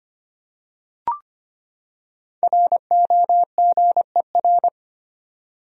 Morse code, heard as a keyed tone, about 25 wpm, 700 Hz. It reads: ROGER
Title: USA Common Names - 25wpm